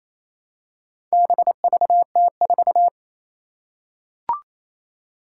Morse code, heard as a keyed tone, about 28 wpm, 700 Hz. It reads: BVT4
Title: Four Challenging Letter-Number Combinations 28wpm